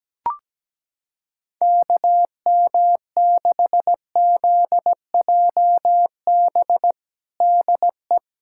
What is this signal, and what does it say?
Morse code, heard as a keyed tone, about 17 wpm, 700 Hz. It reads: KM6ZJB DE
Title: USA Callsign DE Callsign 17wpm